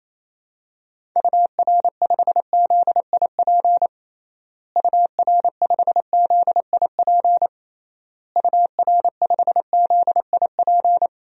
Morse code, heard as a keyed tone, about 28 wpm, 700 Hz. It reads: UR5ZIP UR5ZIP UR5ZIP
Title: DXCC-100 Call Signs - 6 Characters - Repeated 3x - 28wpm